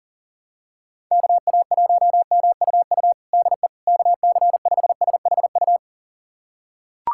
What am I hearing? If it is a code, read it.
Morse code, heard as a keyed tone, about 40 wpm, 700 Hz. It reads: KA1MUU DE KC5SHU
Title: USA Callsign DE Callsign - 6 Characters - 40wpm